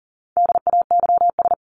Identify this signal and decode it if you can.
Morse code, heard as a keyed tone, about 37 wpm, 700 Hz. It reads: DAYS
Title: ES:S Suffix - Top 100 Words with Base Words 40wpm